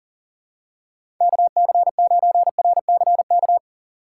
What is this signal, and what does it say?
Morse code, heard as a keyed tone, about 40 wpm, 700 Hz. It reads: KC9RCK
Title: USA Call Signs - 6 Characters - 4x Character Spacing - 40wpm